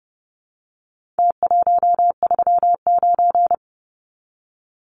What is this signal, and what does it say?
Morse code, heard as a keyed tone, about 30 wpm, 700 Hz. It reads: T139
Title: Numbers - Contest Style - 4-Digit 30wpm